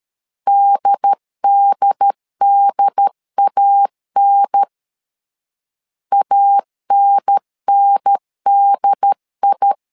Morse code, heard as a keyed tone, about 13 wpm, 800 Hz. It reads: DDDAN ANNDI
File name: LEZIONI / 2 UHDN / 2 UHDN 12wpm 1456